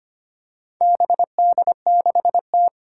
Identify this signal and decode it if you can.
Morse code, heard as a keyed tone, about 25 wpm, 700 Hz. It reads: BD6T
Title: Four Challenging Letter-Number Combinations 25wpm